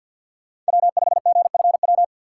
Morse code, heard as a keyed tone, about 51 wpm, 700 Hz. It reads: W5GFW
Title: USA Call Signs - 5 Characters - 4x Character Spacing - 50wpm